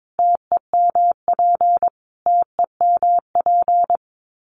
Morse code, heard as a keyed tone, about 22 wpm, 700 Hz. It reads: TEMP TEMP
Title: Abbreviations Repeated 3x - 22wpm